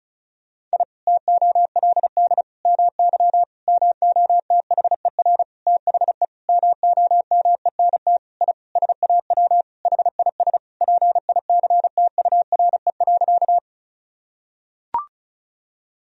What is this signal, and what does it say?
Morse code, heard as a keyed tone, about 34 wpm, 700 Hz. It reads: I TOLD MY MOTHER THE MOMENT I SAW HIS PICTURE.
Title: Sentences from Top 1000 Words - Review Part 1 - 35wpm